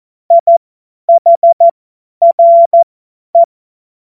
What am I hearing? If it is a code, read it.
Morse code, heard as a keyed tone, about 14 wpm, spaced as for 10 wpm, 680 Hz. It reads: IHRE
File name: Morsekurs / Witze / Witz 17. Januar 2023 50 F70